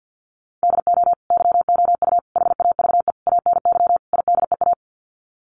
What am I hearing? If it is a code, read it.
Morse code, heard as a keyed tone, about 50 wpm, 700 Hz. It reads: DO YOU HAVE ANY IDEA
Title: Sets of 5 Words Encore 50wpm